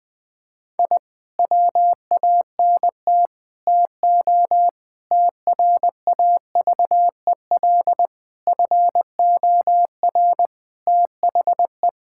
Morse code, heard as a keyed tone, about 20 wpm, 700 Hz. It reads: I WANT TO TRAVEL FOR THE
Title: Sentences with 5-Word Phrases - 20wpm